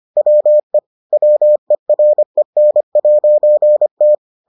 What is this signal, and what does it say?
Morse code, heard as a keyed tone, about 25 wpm, 600 Hz. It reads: WE WEREN'T